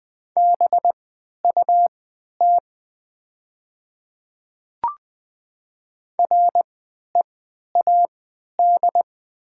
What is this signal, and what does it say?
Morse code, heard as a keyed tone, about 20 wpm, spaced as for 11 wpm, 700 Hz. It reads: BUT READ
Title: 3x Character Spacing - Top 300 Words - 20wpm